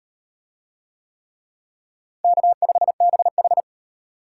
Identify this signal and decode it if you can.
Morse code, heard as a keyed tone, about 38 wpm, 700 Hz. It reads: K5BH